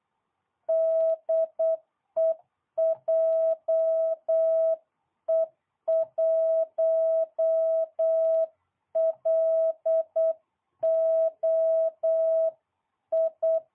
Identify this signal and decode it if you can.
Morse code, heard as a keyed tone, about 8 wpm, 650 Hz. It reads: DE JE1LOI